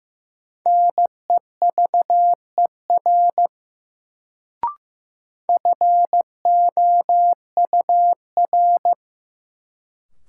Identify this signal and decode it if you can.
Morse code, heard as a keyed tone, about 15 wpm, 700 Hz. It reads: NEVER FOUR
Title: Sentences from Top 300 Words - New Word Drill - 15wpm